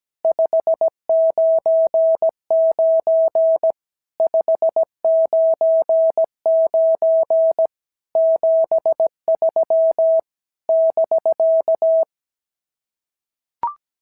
Morse code, heard as a keyed tone, about 17 wpm, 650 Hz. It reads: 599 599 73 <BK>
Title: SOTA QSO Practice 17wpm